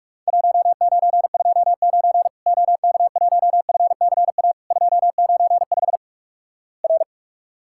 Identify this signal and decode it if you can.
Morse code, heard as a keyed tone, about 46 wpm, 700 Hz. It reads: 1929 OK1FCA 295 R
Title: Worked All Europe DX Contest - QTC Exchange - 45wpm